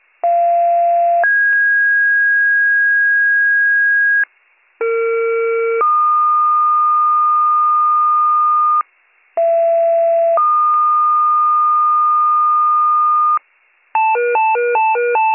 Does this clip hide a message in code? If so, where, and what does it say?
Yes: two-tone page from 0.2 to 13.4 s, heard as a one-second tone, then a three-second tone, about 690 then 1650 Hz, about 460 then 1220 Hz, about 670 then 1220 Hz.